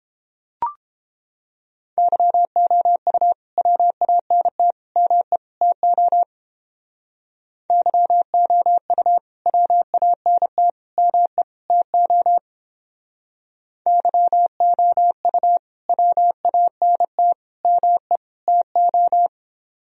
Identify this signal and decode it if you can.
Morse code, heard as a keyed tone, about 29 wpm, 700 Hz. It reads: YOU WANT ME TO YOU WANT ME TO YOU WANT ME TO